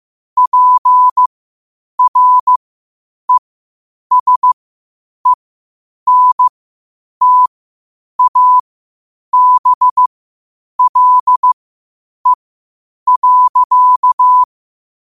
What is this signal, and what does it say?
Morse code, heard as a keyed tone, about 15 wpm, spaced as for 8 wpm, 1000 Hz. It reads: PRESENTABLE.
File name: qotdfiles / 8wpm